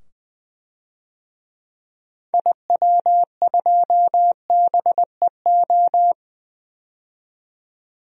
Morse code, heard as a keyed tone, about 20 wpm, 700 Hz. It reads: IW2BEO
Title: DXCC-100 Call Signs - 6 Characters - 20wpm